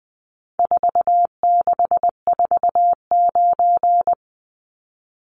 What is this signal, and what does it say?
Morse code, heard as a keyed tone, about 20 wpm, 700 Hz. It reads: 4649
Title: Numbers - 4-Digit 20wpm